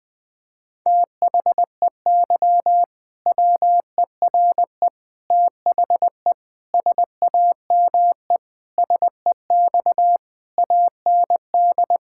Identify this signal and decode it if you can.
Morse code, heard as a keyed tone, about 21 wpm, 700 Hz. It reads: THEY WERE THE SAME SEX AND
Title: Sentences from Top 1600 Words 20wpm